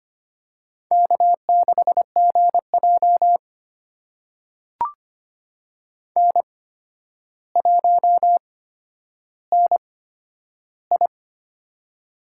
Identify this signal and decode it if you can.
Morse code, heard as a keyed tone, about 25 wpm, spaced as for 11 wpm, 700 Hz. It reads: K6GJ N1NI
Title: USA Call Signs - 4 Characters - 8x Character Spacing - 25wpm